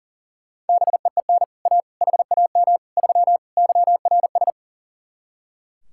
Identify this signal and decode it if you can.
Morse code, heard as a keyed tone, about 50 wpm, 700 Hz. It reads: BEEN A HAM 3 YRS
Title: Speed-Racing Common QSO Phrases Encore - 40wpm